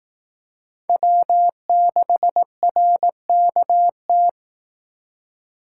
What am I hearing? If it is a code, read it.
Morse code, heard as a keyed tone, about 18 wpm, 700 Hz. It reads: W6RKT
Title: Speed Racing - USA Call Signs 15wpm